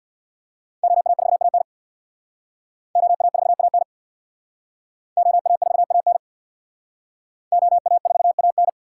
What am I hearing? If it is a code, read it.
Morse code, heard as a keyed tone, about 60 wpm, 700 Hz. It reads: OA4AN OA4AN OA4AN OA4AN
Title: Speed Racing - USA 2x2 Call Signs 50wpm